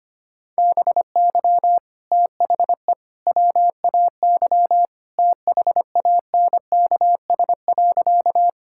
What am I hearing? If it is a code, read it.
Morse code, heard as a keyed tone, about 25 wpm, 700 Hz. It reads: BY THE WAY THANKS.
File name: practice / 25wpm / Sentences with 3-Word Phrases - 25wpm